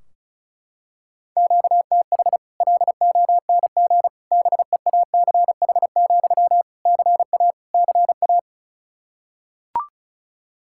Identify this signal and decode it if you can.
Morse code, heard as a keyed tone, about 36 wpm, 700 Hz. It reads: QTH LONG BEACH, CA CA
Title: Practice QSO - Exchange 1 - Beginning - 35wpm